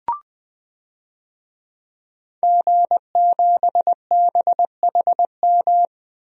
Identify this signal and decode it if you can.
Morse code, heard as a keyed tone, about 20 wpm, 700 Hz. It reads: G7BHM
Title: DXCC-100 Call Signs - 5 Characters - Repeated 3x - 20wpm